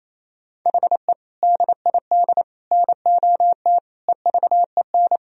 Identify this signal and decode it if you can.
Morse code, heard as a keyed tone, about 28 wpm, 700 Hz. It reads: HE DID NOT EVEN
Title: Sentences from Top 1000 Words - Review Part 1 - 28wpm